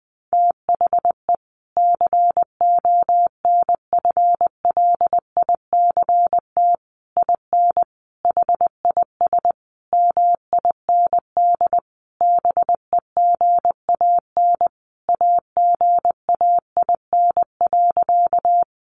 Morse code, heard as a keyed tone, about 20 wpm, 700 Hz. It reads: THE CONFLICT IN HIS MIND BEGAN AGAIN.
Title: Sentences from Top 1300 Words 20wpm